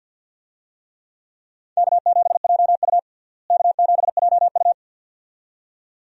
Morse code, heard as a keyed tone, about 51 wpm, 700 Hz. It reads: K7JU K7JU
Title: USA Call Signs - 4 Characters - Repeated 2x - 50wpm